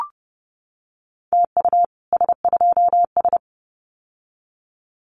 Morse code, heard as a keyed tone, about 30 wpm, 700 Hz. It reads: TU S2S
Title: Common QSO Phrases Encore 30wpm